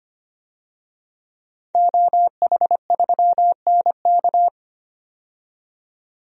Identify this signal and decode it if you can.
Morse code, heard as a keyed tone, about 25 wpm, 700 Hz. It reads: OH3NK
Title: DXCC-100 Call Signs - 5 Characters - 25wpm